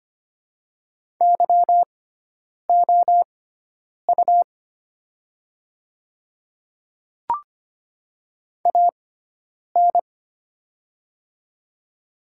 Morse code, heard as a keyed tone, about 25 wpm, spaced as for 9 wpm, 700 Hz. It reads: YOU AN
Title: 6x Character Spacing - Top 100 Words - 25wpm